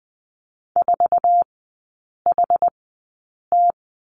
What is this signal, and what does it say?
Morse code, heard as a keyed tone, about 20 wpm, spaced as for 8 wpm, 700 Hz. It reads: 4HT
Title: 074 Lesson 13e - Letter L - Three Characters 20wpm